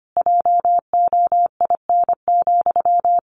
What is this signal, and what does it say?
Morse code, heard as a keyed tone, about 25 wpm, 700 Hz. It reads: JOIN,
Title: Sentences from Top 1400 Words 25wpm